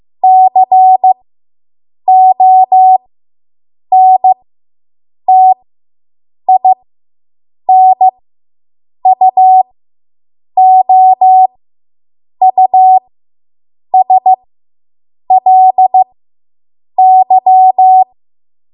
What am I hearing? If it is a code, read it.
Morse code, heard as a keyed tone, about 15 wpm, spaced as for 7 wpm, 750 Hz. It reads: CONTINUOUSLY